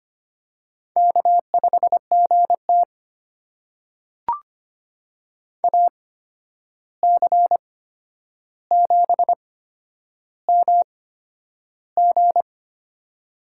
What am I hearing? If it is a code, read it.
Morse code, heard as a keyed tone, about 25 wpm, spaced as for 10 wpm, 700 Hz. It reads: K5GT AC7MG
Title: USA Call Signs - 8x Character Spacing - 25wpm